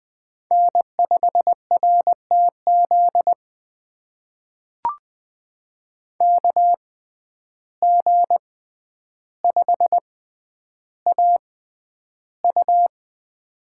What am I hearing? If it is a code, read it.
Morse code, heard as a keyed tone, about 20 wpm, spaced as for 10 wpm, 700 Hz. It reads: N5RTZ KG5AU
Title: USA Call Signs - 5 Characters - 6x Character Spacing - 20wpm